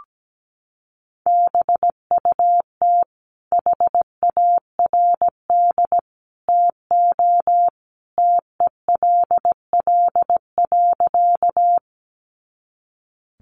Morse code, heard as a keyed tone, about 17 wpm, 700 Hz. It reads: BUT HARD TO TELL.
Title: Sentences from Top 300 Words 17wpm